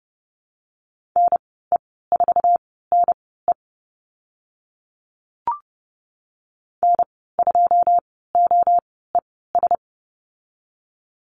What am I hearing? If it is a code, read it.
Morse code, heard as a keyed tone, about 30 wpm, spaced as for 17 wpm, 700 Hz. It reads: NE4NE N2OES
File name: Lesson 14g - Number 2 - Call Signs 30wpm